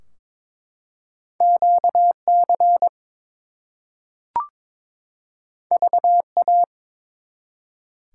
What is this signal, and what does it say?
Morse code, heard as a keyed tone, about 22 wpm, 700 Hz. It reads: QC VA
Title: US States and Canadian Provinces 22wpm